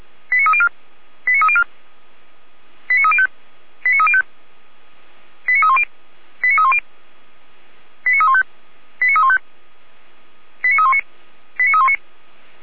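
Five-tone selective call (ZVEI), heard as quick runs of five tones, tones of about 70 ms. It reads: 89304 89304 89305 89305 89310 89310 89315 89315 89319 89319
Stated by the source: ZVEI five-tone call received by SDR, recorded by Feuerwehr Orlamünde (f85360).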